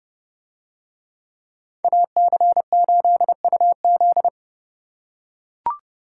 Morse code, heard as a keyed tone, about 30 wpm, 700 Hz. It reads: AC8UZ